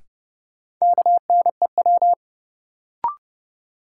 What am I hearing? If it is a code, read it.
Morse code, heard as a keyed tone, about 31 wpm, 700 Hz. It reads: KNEW
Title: Sentences from Top 400 Words - New Word Drill - 30wpm